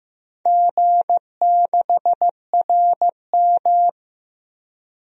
Morse code, heard as a keyed tone, about 15 wpm, 700 Hz. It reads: G6RM
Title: DXCC-100 Call Signs - 4 Characters - Repeated 3x - 15wpm